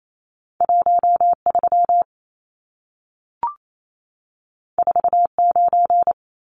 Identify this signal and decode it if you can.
Morse code, heard as a keyed tone, about 28 wpm, 700 Hz. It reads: 13 49
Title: Numbers - 2-Digit 28wpm